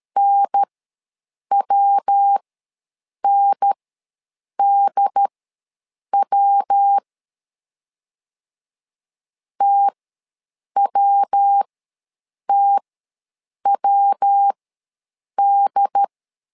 Morse code, heard as a keyed tone, about 13 wpm, spaced as for 7 wpm, 800 Hz. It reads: NWNDW TWTWD
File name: LEZIONI / 3 MWRO / 3 MW ws 12wpm 1816